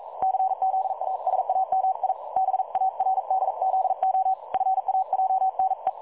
Morse code, heard as a keyed tone, about 42 wpm, 760 Hz. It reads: CZASAMI BARDZO FAJNE
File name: dlazaawansowanych / nagraniazpasma / websdr recording 3565.0kHz